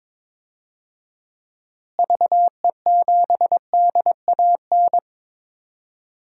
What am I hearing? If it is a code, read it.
Morse code, heard as a keyed tone, about 22 wpm, 700 Hz. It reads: VE7DAN
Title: DXCC-100 Call Signs - 6 Characters - Repeated 2x - 22wpm